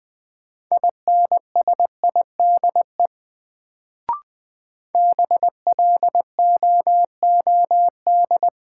Morse code, heard as a keyed tone, about 20 wpm, 700 Hz. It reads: INSIDE BLOOD
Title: Sentences from Top 1000 Words - Review Part 2 - New Word Drill - 20wpm